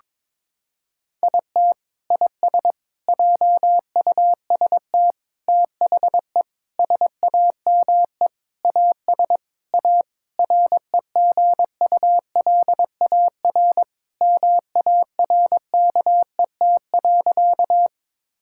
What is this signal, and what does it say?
Morse code, heard as a keyed tone, about 22 wpm, 700 Hz. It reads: IT IS JUST THE SAME AS A REGULAR MARKET.
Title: Sentences from Top 1000 Words - Review Part 6 - 22wpm